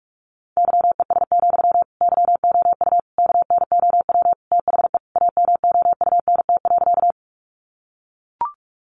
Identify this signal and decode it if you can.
Morse code, heard as a keyed tone, about 44 wpm, 700 Hz. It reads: YES, YOU KNOW THE AMOUNT.
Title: Sentences from Top 700 Words 45wpm